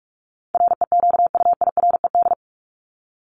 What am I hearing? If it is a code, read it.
Morse code, heard as a keyed tone, about 44 wpm, 700 Hz. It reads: REQUIRED
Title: ED Suffix - Top 100 Words with Base Words 45wpm